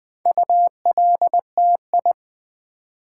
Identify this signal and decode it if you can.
Morse code, heard as a keyed tone, about 20 wpm, 700 Hz. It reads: ULTI
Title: Build A Word - 4 Letters At A Time - 10-Letter Words - 20wpm